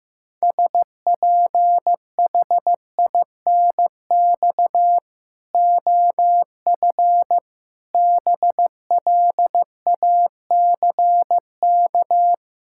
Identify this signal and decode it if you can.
Morse code, heard as a keyed tone, about 15 wpm, 700 Hz. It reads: SPHINX OF BLACK
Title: Pangram Challenge v2 - 15wpm